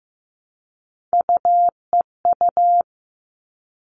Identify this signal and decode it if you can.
Morse code, heard as a keyed tone, about 15 wpm, 700 Hz. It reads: UEU
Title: All Cut Numbers - 3-Digit 15wpm